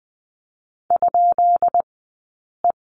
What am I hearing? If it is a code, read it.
Morse code, heard as a keyed tone, about 20 wpm, spaced as for 8 wpm, 700 Hz. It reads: ?E
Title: 151 Lesson 25d - Letter p - Two Characters 20wpm